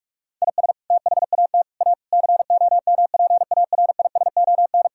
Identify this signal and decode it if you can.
Morse code, heard as a keyed tone, about 46 wpm, 700 Hz. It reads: IS THAT A COMPARISON
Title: Sentences from Top 1600 Words 45wpm